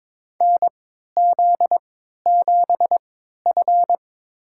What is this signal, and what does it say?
Morse code, heard as a keyed tone, about 21 wpm, spaced as for 12 wpm, 700 Hz. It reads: NZ7F
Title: USA Call Signs - 4 Characters - 3x Character Spacing - 22wpm